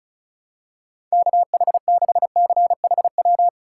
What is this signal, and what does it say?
Morse code, heard as a keyed tone, about 36 wpm, 700 Hz. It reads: KH6CHW
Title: USA Call Signs - 6 Characters - 4x Character Spacing - 35wpm